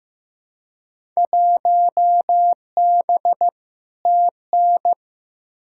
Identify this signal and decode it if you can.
Morse code, heard as a keyed tone, about 15 wpm, 700 Hz. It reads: 1B TN
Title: Field Day Exchanges - 15wpm